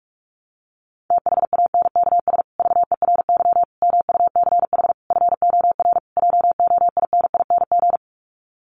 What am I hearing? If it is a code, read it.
Morse code, heard as a keyed tone, about 44 wpm, 700 Hz. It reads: THANKS VERY MUCH FOR JOINING
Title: Sets of 5 Words Encore2 45wpm